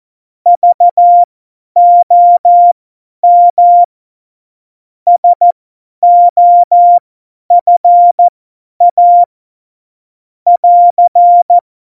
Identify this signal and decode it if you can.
Morse code, heard as a keyed tone, about 14 wpm, spaced as for 10 wpm, 700 Hz. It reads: VOM SOFA <AR>
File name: Morsekurs / Witze / Witz 12. Juni 2018, 50 F70